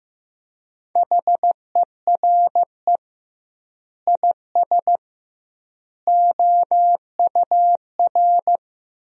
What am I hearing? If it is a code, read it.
Morse code, heard as a keyed tone, about 15 wpm, 700 Hz. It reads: HERE IS OUR
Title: 2x Word-Spacing - Sentences from Top 500 Words - 15wpm